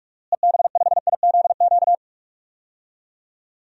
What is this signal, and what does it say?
Morse code, heard as a keyed tone, about 46 wpm, 700 Hz. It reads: EB5IZQ
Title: DXCC-100 Call Signs - 6 Characters - Repeated 3x - 45wpm